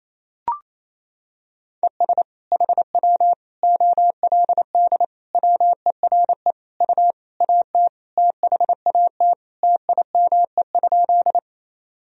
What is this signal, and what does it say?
Morse code, heard as a keyed tone, about 28 wpm, 700 Hz. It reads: ES HW OLD WERE U AT THAT TIME?
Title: Sentences from Top 200 Words and Abbreviations 28wpm